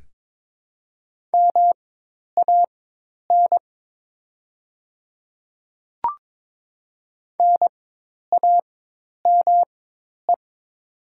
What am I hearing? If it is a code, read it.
Morse code, heard as a keyed tone, about 22 wpm, spaced as for 10 wpm, 700 Hz. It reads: MAN NAME
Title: 4x Character Spacing - Top 300 Words - 22wpm